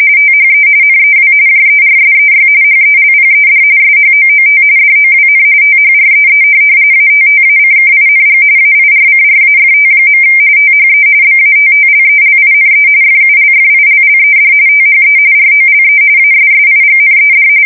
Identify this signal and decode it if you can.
RTTY, about 45.45 baud. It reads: THE NEW JERSEY HELPED SHOOT DOWN THREE ATTACKING AIRCRAFT. THE BATTLELINE FULFILLED THE ROLE ASSIGNED
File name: fsk170 / USS-New-Jersey